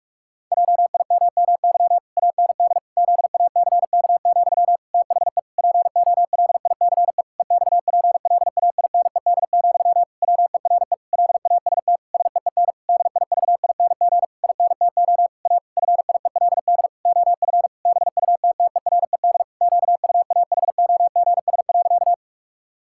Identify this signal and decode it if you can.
Morse code, heard as a keyed tone, about 46 wpm, 700 Hz. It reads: JIMMY AND ZACK, THE POLICE EXPLAINED, WERE LAST SEEN DIVING INTO A FIELD OF BUTTERED QUAHOGS.
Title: Pangram Challenge v2 - 45wpm